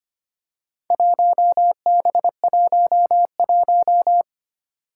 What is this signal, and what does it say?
Morse code, heard as a keyed tone, about 25 wpm, 700 Hz. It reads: 1B11
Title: Four Challenging Letter-Number Combinations 25wpm